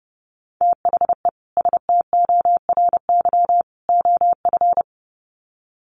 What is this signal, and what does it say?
Morse code, heard as a keyed tone, about 30 wpm, 700 Hz. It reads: THE STORY OF
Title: Sets of 3 Words 30wpm